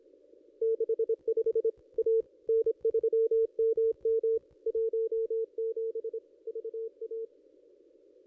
Morse code, heard as a keyed tone, about 26 wpm, 450 Hz. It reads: 65 A N3MM 17 VA
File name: contests / ss-cw-k5tr-op-n5rz -1800